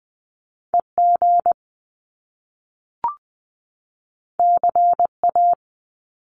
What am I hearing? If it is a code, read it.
Morse code, heard as a keyed tone, about 20 wpm, 700 Hz. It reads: EG CA
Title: Top Two Letter Combinations in Words 20wpm